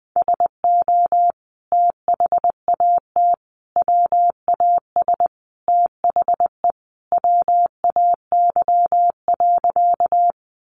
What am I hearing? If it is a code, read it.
Morse code, heard as a keyed tone, about 20 wpm, 700 Hz. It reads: SO THAT WAS THE WAY.
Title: Sentences from Top 100 Words 20wpm